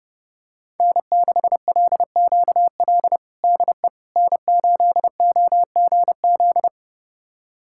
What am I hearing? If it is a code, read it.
Morse code, heard as a keyed tone, about 30 wpm, 700 Hz. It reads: N6LQL DE N8OGZ
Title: USA Callsign DE Callsign - 5 Characters - 30wpm